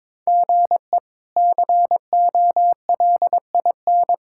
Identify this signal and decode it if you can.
Morse code, heard as a keyed tone, about 22 wpm, 700 Hz. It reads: GE COLIN
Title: K1USN SST QSO Practice - 22wpm